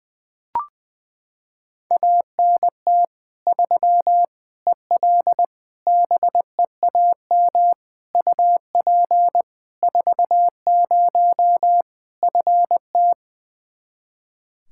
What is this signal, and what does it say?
Morse code, heard as a keyed tone, about 20 wpm, 700 Hz. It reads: ANT 3 EL BEAM UP 40 FT
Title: Practice QSO - Exchange 2 - Info - 20wpm